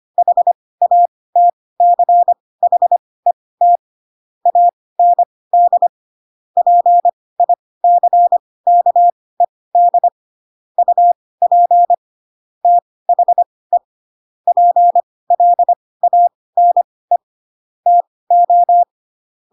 Morse code, heard as a keyed tone, about 25 wpm, spaced as for 18 wpm, 700 Hz. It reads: HATCHET AND PICKED UP THE PLANE TO
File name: Pinocchio / Pinocchio ch1 12 25 ik1hge